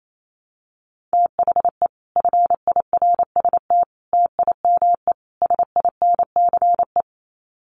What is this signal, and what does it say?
Morse code, heard as a keyed tone, about 28 wpm, 700 Hz. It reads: THE FIRST TIME SINCE
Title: Sets of 4 Words 28wpm